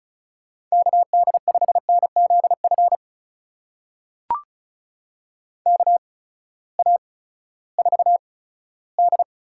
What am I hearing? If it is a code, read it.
Morse code, heard as a keyed tone, about 36 wpm, spaced as for 18 wpm, 700 Hz. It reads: KD5NZF KA4D
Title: USA Call Signs - 8x Character Spacing - 35wpm